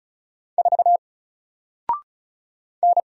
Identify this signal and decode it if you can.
Morse code, heard as a keyed tone, about 36 wpm, 700 Hz. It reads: 4 N
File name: Single-Letter-Number-35wpm